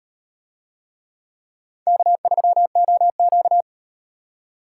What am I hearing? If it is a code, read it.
Morse code, heard as a keyed tone, about 38 wpm, 700 Hz. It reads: K3OQ